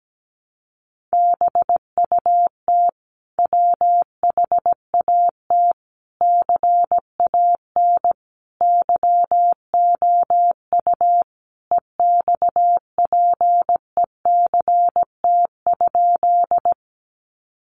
Morse code, heard as a keyed tone, about 17 wpm, 700 Hz. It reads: BUT WHAT CAN YOU EXPECT?
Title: Sentences from Top 1100 Words 17wpm